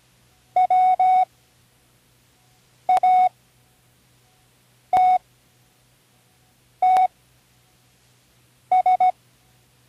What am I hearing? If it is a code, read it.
Morse code, heard as a keyed tone, about 17 wpm, spaced as for 5 wpm, 740 Hz. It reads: WATTS
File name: Learning Morse Code (from Tune in the World)